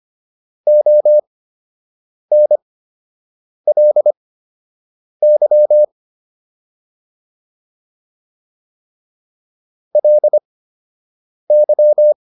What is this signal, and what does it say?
Morse code, heard as a keyed tone, about 25 wpm, spaced as for 7 wpm, 600 Hz. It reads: ONLY LY